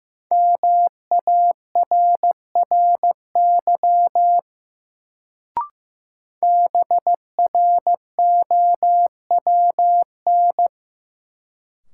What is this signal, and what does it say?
Morse code, heard as a keyed tone, about 16 wpm, 700 Hz. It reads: MARRY BROWN
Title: Sentences from Top 1200 Words - New Word Drill Alt - 15wpm